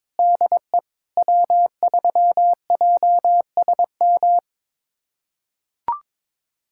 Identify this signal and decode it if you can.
Morse code, heard as a keyed tone, about 21 wpm, 700 Hz. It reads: DE W3JSM
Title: USA Callsign DE Callsign - 5 Characters - 22wpm